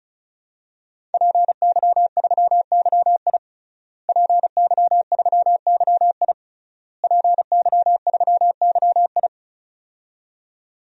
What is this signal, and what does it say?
Morse code, heard as a keyed tone, about 36 wpm, 700 Hz. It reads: PY3YI PY3YI PY3YI
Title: DXCC-100 Call Signs - 5 Characters - Repeated 3x - 35wpm